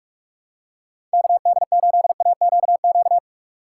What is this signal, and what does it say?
Morse code, heard as a keyed tone, about 46 wpm, 700 Hz. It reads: KD8AQQ
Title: USA Call Signs - 6 Characters - 6x Character Spacing - 45wpm